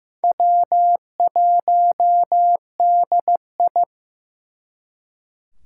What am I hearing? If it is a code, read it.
Morse code, heard as a keyed tone, about 18 wpm, 700 Hz. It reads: W1DI
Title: Speed Racing - USA 1x2 Call Signs 15wpm